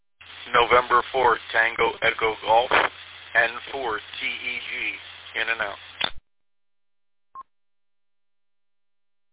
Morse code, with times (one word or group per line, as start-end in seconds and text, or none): none